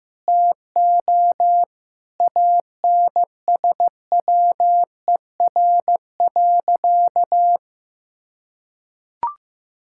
Morse code, heard as a keyed tone, about 15 wpm, 700 Hz. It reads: TO ANSWER.
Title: Sentences from Top 1000 Words - Review Part 6 - 15wpm